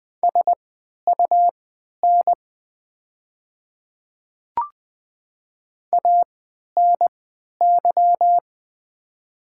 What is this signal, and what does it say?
Morse code, heard as a keyed tone, about 20 wpm, spaced as for 11 wpm, 700 Hz. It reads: SUN ANY
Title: 3x Character Spacing - Top 500 Words - 20wpm